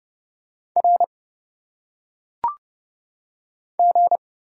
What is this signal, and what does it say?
Morse code, heard as a keyed tone, about 30 wpm, 700 Hz. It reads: R G
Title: Single Letters - Phonetic Pronunciation 30wpm